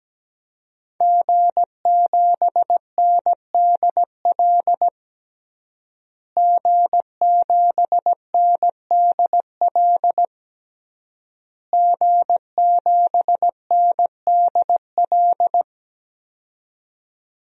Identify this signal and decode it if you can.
Morse code, heard as a keyed tone, about 17 wpm, 700 Hz. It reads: G7NDL G7NDL G7NDL
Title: DXCC-100 Call Signs - 5 Characters - Repeated 3x - 17wpm